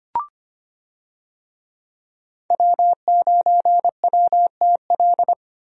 Morse code, heard as a keyed tone, about 25 wpm, 700 Hz. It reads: W9WTL
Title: USA Call Signs - 5 Characters - Repeated 2x - 25wpm